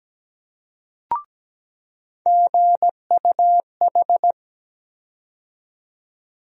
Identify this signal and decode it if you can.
Morse code, heard as a keyed tone, about 17 wpm, 700 Hz. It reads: GUH
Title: USA Call Sign Postfixes - 3 Letters 17wpm